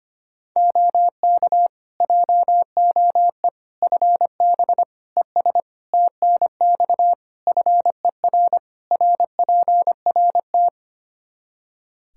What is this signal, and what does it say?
Morse code, heard as a keyed tone, about 25 wpm, 700 Hz. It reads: OK JOE FB ES TNX FER RPRT
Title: K1USN SST QSO Practice 25wpm